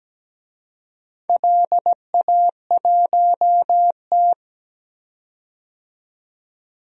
Morse code, heard as a keyed tone, about 17 wpm, 700 Hz. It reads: LA1T
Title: DXCC-100 Call Signs - 4 Characters - 17wpm